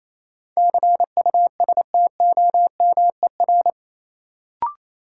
Morse code, heard as a keyed tone, about 28 wpm, 700 Hz. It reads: CUSTOMER
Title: Sentences from Top 1100 Words - New Word Drill - 28wpm